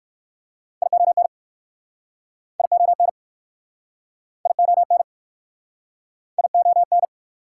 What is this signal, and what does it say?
Morse code, heard as a keyed tone, about 58 wpm, 690 Hz. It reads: ION ION ION ION